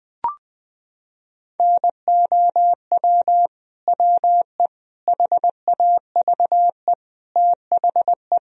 Morse code, heard as a keyed tone, about 20 wpm, 700 Hz. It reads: NOW WE HAVE THE
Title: Sentences from Top 1000 Words - Review Part 3 - 20wpm